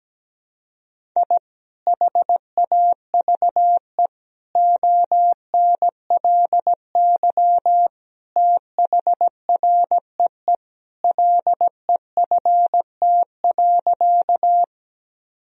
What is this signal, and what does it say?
Morse code, heard as a keyed tone, about 17 wpm, 700 Hz. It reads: I HAVE ONLY THREE LEFT.
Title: Sentences from Top 1000 Words - Review Part 1 - 17wpm